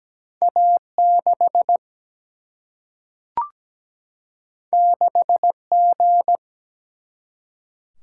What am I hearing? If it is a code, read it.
Morse code, heard as a keyed tone, about 17 wpm, 700 Hz. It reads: A 6 6 G
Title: Single-Letter Single-Number 17wpm